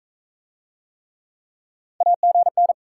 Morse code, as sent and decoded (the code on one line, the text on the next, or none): .- --. -.
AGN